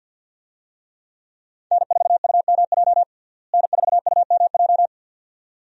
Morse code, heard as a keyed tone, about 46 wpm, 700 Hz. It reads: N4UMJ N4UMJ
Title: USA Call Signs - 5 Characters - Repeated 2x - 50wpm